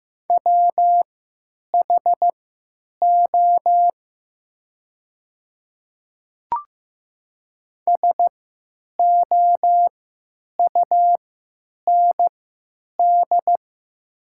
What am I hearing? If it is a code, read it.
Morse code, heard as a keyed tone, about 15 wpm, spaced as for 9 wpm, 700 Hz. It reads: WHO SOUND
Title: 3x Character Spacing - Top 300 Words - 15wpm